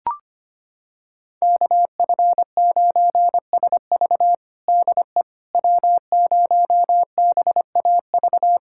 Morse code, heard as a keyed tone, about 25 wpm, 700 Hz. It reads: KF9SV DE W0BAV
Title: USA Callsign DE Callsign - 5 Characters - 25wpm